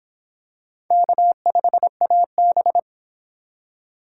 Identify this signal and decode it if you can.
Morse code, heard as a keyed tone, about 24 wpm, 700 Hz. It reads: K5AB